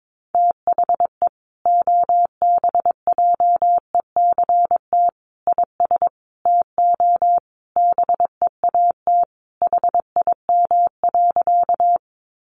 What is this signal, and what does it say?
Morse code, heard as a keyed tone, about 22 wpm, 700 Hz. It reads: THE OBJECT IS TO BEAT HIM.
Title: Sentences from Top 1300 Words 22wpm